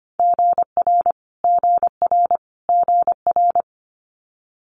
Morse code, heard as a keyed tone, about 25 wpm, 700 Hz. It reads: GR GR GR
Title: Multi-letter Phonemes Repeated 3x 25wpm